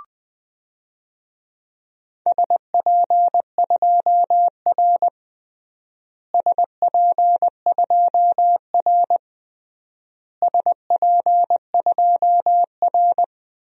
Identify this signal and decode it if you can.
Morse code, heard as a keyed tone, about 20 wpm, 700 Hz. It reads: SP2R SP2R SP2R
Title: DXCC-100 Call Signs - 4 Characters - Repeated 3x - 20wpm